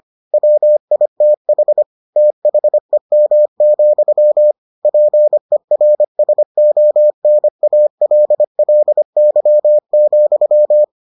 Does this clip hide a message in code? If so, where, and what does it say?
Yes: Morse code from 0.3 to 10.8 s, heard as a keyed tone, about 25 wpm, 600 Hz. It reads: WITH THEM, PERSONALLY,